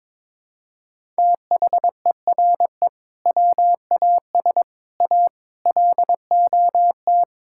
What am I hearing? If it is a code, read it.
Morse code, heard as a keyed tone, about 21 wpm, 700 Hz. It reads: THERE WAS A LOT
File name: practice / 22wpm / Build A Phrase - 5-Word Phrases - 22wpm